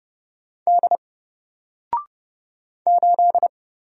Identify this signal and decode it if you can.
Morse code, heard as a keyed tone, about 30 wpm, 700 Hz. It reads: D 8
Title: Single Character (Letter:Number:Punctuation:Prosign) 30wpm